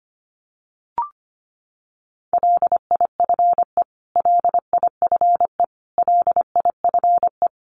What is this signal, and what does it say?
Morse code, heard as a keyed tone, about 25 wpm, 700 Hz. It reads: LIFE LIFE LIFE
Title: Top 500 Words Repeated 25wpm